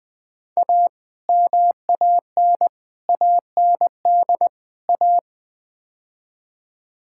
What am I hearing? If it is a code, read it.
Morse code, heard as a keyed tone, about 21 wpm, 700 Hz. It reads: A MAN AND A
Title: Build A Phrase - 5-Word Phrases - 20wpm